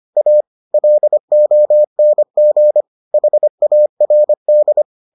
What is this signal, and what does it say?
Morse code, heard as a keyed tone, about 25 wpm, 600 Hz. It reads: A LONG HARD